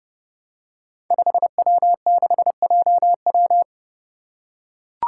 Morse code, heard as a keyed tone, about 30 wpm, 700 Hz. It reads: 5W6JW
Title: Five Challenging Letter-Number Combinations 30wpm